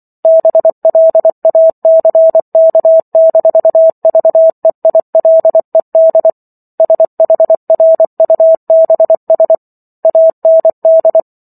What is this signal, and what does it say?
Morse code, heard as a keyed tone, about 24 wpm, 650 Hz. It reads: BLACK-VEILED SHRUBS AND